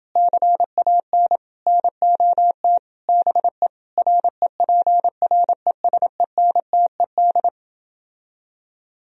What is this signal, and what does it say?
Morse code, heard as a keyed tone, about 30 wpm, 700 Hz. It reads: CAN NOT BE REPRESENTED